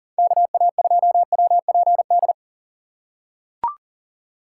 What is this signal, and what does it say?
Morse code, heard as a keyed tone, about 37 wpm, 700 Hz. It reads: KA2WPD
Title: USA Call Signs - 6 Characters - 6x Character Spacing - 40wpm